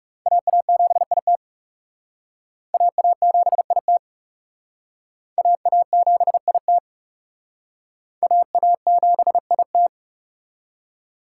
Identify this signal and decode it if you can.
Morse code, heard as a keyed tone, about 36 wpm, 700 Hz. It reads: AA7IT AA7IT AA7IT AA7IT
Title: Speed Racing - USA 2x2 Call Signs 30wpm